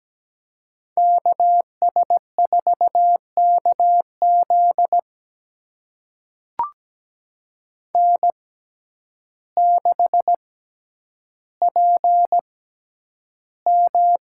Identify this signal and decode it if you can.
Morse code, heard as a keyed tone, about 17 wpm, spaced as for 9 wpm, 700 Hz. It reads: KS4KZ N6PM
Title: USA Call Signs - 5 Characters - 6x Character Spacing - 17wpm